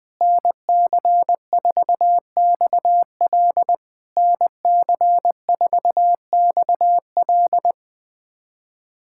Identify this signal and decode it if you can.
Morse code, heard as a keyed tone, about 20 wpm, 690 Hz. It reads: NC4XL NC4XL
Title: POTA QSO Practice - 20wpm